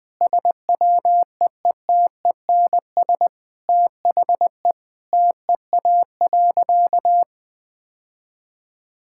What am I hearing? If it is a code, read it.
Morse code, heard as a keyed tone, about 21 wpm, 700 Hz. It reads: SWEETENS THE TEA.
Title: Sentences with 2-Word Phrases - Spoken Beginning - 20wpm